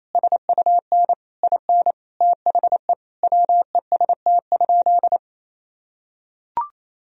Morse code, heard as a keyed tone, about 28 wpm, 700 Hz. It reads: SUN IN THE WEST?
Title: Sentences from Top 1000 Words - Review Part 2 - 28wpm